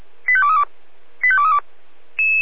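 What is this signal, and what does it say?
Five-tone selective call (ZVEI), heard as quick runs of five tones, tones of about 70 ms. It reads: 85232 85232
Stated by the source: ZVEI five-tone selective call 85355 received via SDR.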